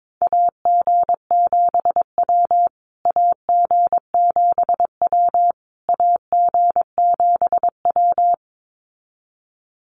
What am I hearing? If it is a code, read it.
Morse code, heard as a keyed tone, about 22 wpm, 700 Hz. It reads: AG7W AG7W AG7W
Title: USA Call Signs Repeated 3x 22wpm